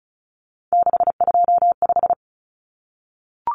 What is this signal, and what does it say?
Morse code, heard as a keyed tone, about 34 wpm, 700 Hz. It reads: 625
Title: Numbers - 3-Digit 35wpm